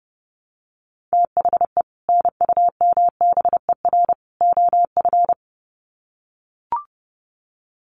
Morse code, heard as a keyed tone, about 30 wpm, 700 Hz. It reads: THE NUMBER OF
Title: Sets of 3 Words 30wpm